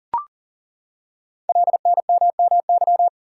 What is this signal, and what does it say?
Morse code, heard as a keyed tone, about 42 wpm, 700 Hz. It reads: LNMMY
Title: Code Groups - 5 Letters - 40wpm